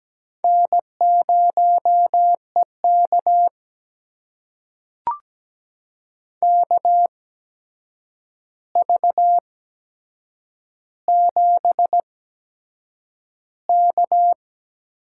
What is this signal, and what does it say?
Morse code, heard as a keyed tone, about 17 wpm, spaced as for 7 wpm, 700 Hz. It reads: N0EK KV7K
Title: USA Call Signs - 4 Characters - 8x Character Spacing - 17wpm